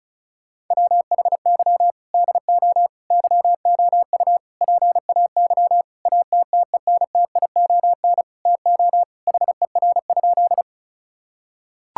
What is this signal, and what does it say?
Morse code, heard as a keyed tone, about 36 wpm, 700 Hz. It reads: WHY DO YOU PAY ATTENTION TO HER?
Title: Sentences from Top 1000 Words - Review Part 6 - 35wpm